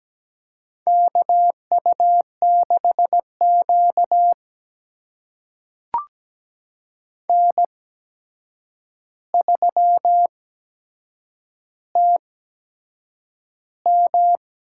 Morse code, heard as a keyed tone, about 17 wpm, spaced as for 7 wpm, 700 Hz. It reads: KU6Q N3TM
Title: USA Call Signs - 4 Characters - 8x Character Spacing - 17wpm